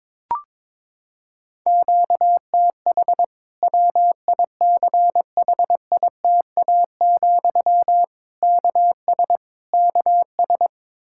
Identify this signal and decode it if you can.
Morse code, heard as a keyed tone, about 22 wpm, 700 Hz. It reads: QTH WICHITA, KS KS
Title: Practice QSO - Exchange 1 - Beginning - 22wpm